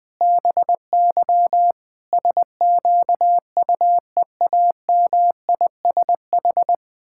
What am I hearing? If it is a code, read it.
Morse code, heard as a keyed tone, about 20 wpm, 700 Hz. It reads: BY SQUEAMISH
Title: Pangram Challenge v2 - 20wpm